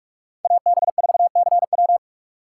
Morse code, heard as a keyed tone, about 46 wpm, 700 Hz. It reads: AB4CW
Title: USA Call Signs - 2x Character Spacing - 45wpm